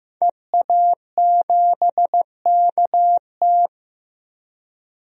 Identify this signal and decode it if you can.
Morse code, heard as a keyed tone, about 15 wpm, 700 Hz. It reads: EA7KT
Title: DXCC-100 Call Signs - 5 Characters - Repeated 3x - 15wpm